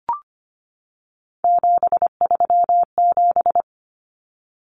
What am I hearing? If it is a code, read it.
Morse code, heard as a keyed tone, about 25 wpm, 700 Hz. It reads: 737
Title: Numbers - 3-Digit 25wpm